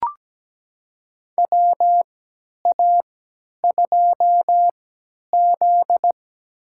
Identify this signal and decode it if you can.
Morse code, heard as a keyed tone, about 17 wpm, spaced as for 10 wpm, 700 Hz. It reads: WA2Z
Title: USA Call Signs - 4 Characters - 3x Character Spacing - 17wpm